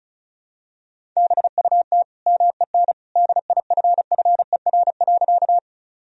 Morse code, heard as a keyed tone, about 34 wpm, 700 Hz. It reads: BUT MEN DIFFER.
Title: Sentences from Top 1000 Words - Review Part 1 - 35wpm